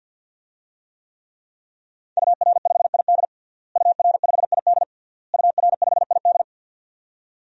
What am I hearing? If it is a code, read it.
Morse code, heard as a keyed tone, about 52 wpm, 700 Hz. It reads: UR5ID UR5ID UR5ID
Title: DXCC-100 Call Signs - 5 Characters - Repeated 3x - 50wpm